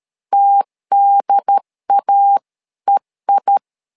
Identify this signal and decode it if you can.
Morse code, heard as a keyed tone, about 13 wpm, 800 Hz. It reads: TDAEI
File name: LEZIONI / 2 UHDN / 2 UHDN 12wpm 1456